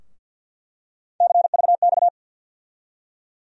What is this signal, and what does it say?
Morse code, heard as a keyed tone, about 47 wpm, 700 Hz. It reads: CVX
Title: USA Call Sign Postfixes - 3 Letters 50wpm